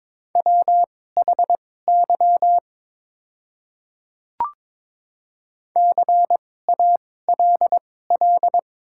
Morse code, heard as a keyed tone, about 22 wpm, spaced as for 16 wpm, 690 Hz. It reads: WHY CALL
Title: 2x Character Spacing - Top 200 Words - 22wpm